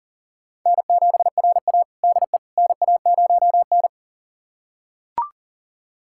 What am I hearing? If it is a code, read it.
Morse code, heard as a keyed tone, about 38 wpm, 700 Hz. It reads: N7RA DE NA0N
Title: USA Callsign DE Callsign - 4 Characters - 40wpm